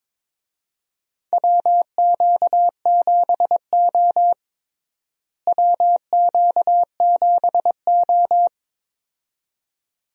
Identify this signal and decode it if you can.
Morse code, heard as a keyed tone, about 22 wpm, 700 Hz. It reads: WQ7O WQ7O
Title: USA Call Signs - 4 Characters - Repeated 2x - 22wpm